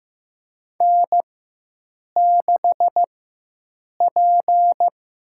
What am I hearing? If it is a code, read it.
Morse code, heard as a keyed tone, about 15 wpm, 700 Hz. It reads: N 6 P
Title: USA Call Signs - 4 Characters - 4x Character Spacing - 15wpm